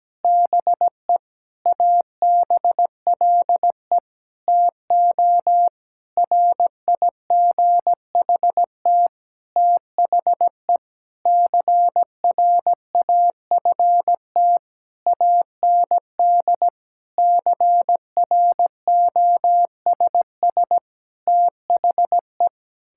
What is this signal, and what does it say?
Morse code, heard as a keyed tone, about 17 wpm, 700 Hz. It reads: BE ABLE TO RIGHT THE CRAFT AND CROSS THE
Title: Sentences from Top 1000 Words - Review Part 6 - 17wpm